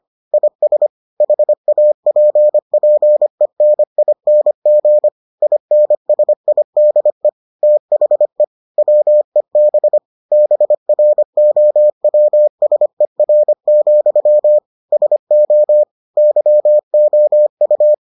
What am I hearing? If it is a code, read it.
Morse code, heard as a keyed tone, about 25 wpm, 600 Hz. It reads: IS HAPPENING INSIDE THE WEB BROWSER, SO YOU